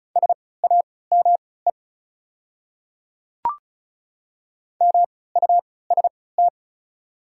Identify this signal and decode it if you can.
Morse code, heard as a keyed tone, about 36 wpm, spaced as for 20 wpm, 700 Hz. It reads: SAME MUST
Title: 3x Character Spacing - Top 300 Words - 35wpm